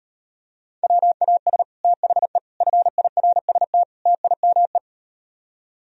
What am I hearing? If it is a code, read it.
Morse code, heard as a keyed tone, about 38 wpm, 700 Hz. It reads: WAS THE FIRST TIME